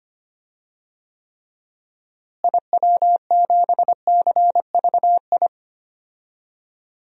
Morse code, heard as a keyed tone, about 25 wpm, 700 Hz. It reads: IW7CVI
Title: DXCC-100 Call Signs - 6 Characters - Repeated 3x - 25wpm